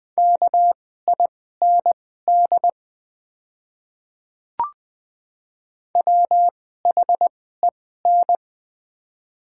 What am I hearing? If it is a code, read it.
Morse code, heard as a keyed tone, about 20 wpm, spaced as for 15 wpm, 700 Hz. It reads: KIND WHEN
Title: 2x Character Spacing - Top 200 Words - 20wpm